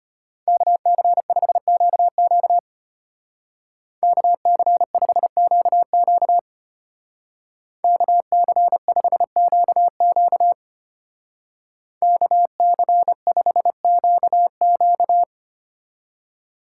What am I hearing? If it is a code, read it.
Morse code, heard as a keyed tone, about 31 wpm, 700 Hz. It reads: KC5QQ KC5QQ KC5QQ KC5QQ